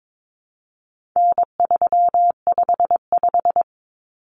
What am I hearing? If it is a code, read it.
Morse code, heard as a keyed tone, about 22 wpm, 700 Hz. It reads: N355
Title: Common Cut Numbers - 4-Digit 22wpm